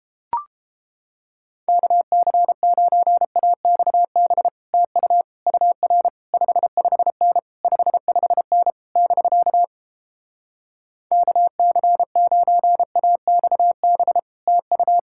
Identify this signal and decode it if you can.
Morse code, heard as a keyed tone, about 29 wpm, 700 Hz. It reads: KC9AXB TU UR 55N 55N <BK> KC9AXB TU